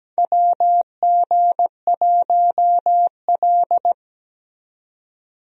Morse code, heard as a keyed tone, about 17 wpm, 700 Hz. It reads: WG1L
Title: USA Call Signs - 4 Characters - 17wpm